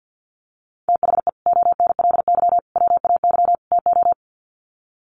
Morse code, heard as a keyed tone, about 50 wpm, 700 Hz. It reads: THE ONLY WAY TO
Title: Sets of 4 Words 50wpm